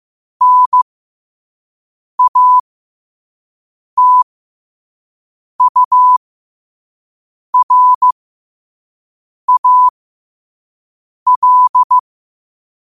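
Morse code, heard as a keyed tone, about 15 wpm, spaced as for 5 wpm, 1000 Hz. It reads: NATURAL
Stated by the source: Morse code practice at 5 wpm.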